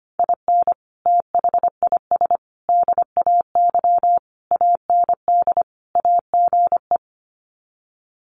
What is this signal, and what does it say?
Morse code, heard as a keyed tone, about 25 wpm, 700 Hz. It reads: IN THIS DAY AND AGE
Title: Sets of 5 Words Encore 25wpm